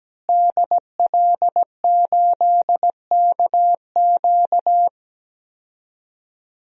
Morse code, heard as a keyed tone, about 17 wpm, 700 Hz. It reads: DL8KQ
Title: DXCC-100 Call Signs - 5 Characters - Repeated 3x - 17wpm